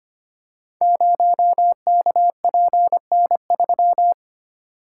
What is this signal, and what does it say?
Morse code, heard as a keyed tone, about 25 wpm, 700 Hz. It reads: 0KPN3
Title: Code Groups - 5 Letter:Numbers - 25wpm